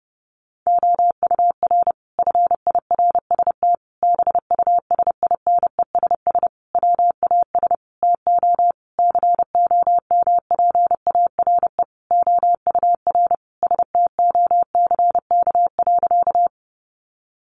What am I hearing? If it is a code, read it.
Morse code, heard as a keyed tone, about 30 wpm, 700 Hz. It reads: OUR FIRST BUSINESS WAS TO COMPARE OUR STOCK.
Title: Sentences from Top 1300 Words 30wpm